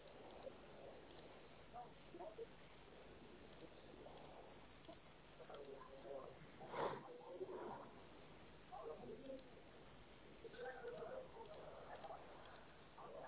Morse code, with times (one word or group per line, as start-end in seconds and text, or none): none